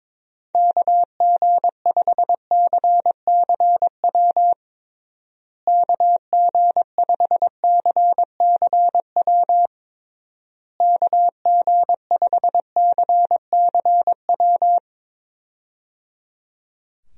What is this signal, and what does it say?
Morse code, heard as a keyed tone, about 21 wpm, 700 Hz. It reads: KG5CCW KG5CCW KG5CCW
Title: USA Call Signs - 6 Characters - Repeated 3x - 22wpm